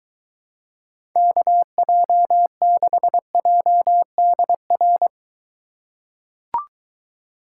Morse code, heard as a keyed tone, about 23 wpm, 700 Hz. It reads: KJ6JDR